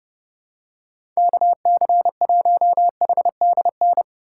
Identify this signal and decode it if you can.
Morse code, heard as a keyed tone, about 32 wpm, 700 Hz. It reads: KC1HDN
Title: USA Call Signs - 6 Characters - 2x Character Spacing - 30wpm